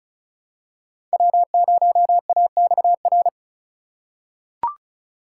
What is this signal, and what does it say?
Morse code, heard as a keyed tone, about 36 wpm, 700 Hz. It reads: W0AXR
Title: USA Call Signs - 5 Characters - 3x Character Spacing - 35wpm